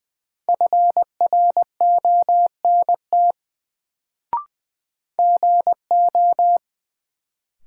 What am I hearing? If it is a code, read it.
Morse code, heard as a keyed tone, about 20 wpm, 700 Hz. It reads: FRONT GO
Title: Sets of 2 Words - New Word Drill - 20wpm